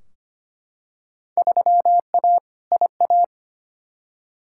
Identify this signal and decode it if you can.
Morse code, heard as a keyed tone, about 25 wpm, 700 Hz. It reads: 3A IA
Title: Field Day Exchanges Encore - 25wpm